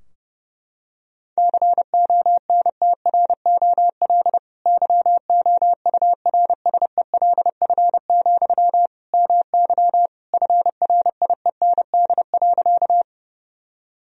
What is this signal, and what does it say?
Morse code, heard as a keyed tone, about 30 wpm, 700 Hz. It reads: CONTROL YOURSELF, MY FRIEND.
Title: Sentences from Top 1000 Words - Review Part 3 - 30wpm